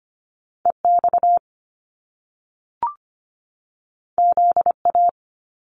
Morse code, heard as a keyed tone, about 25 wpm, 700 Hz. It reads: EX ZA
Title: Top Two Letter Combinations in Words 25wpm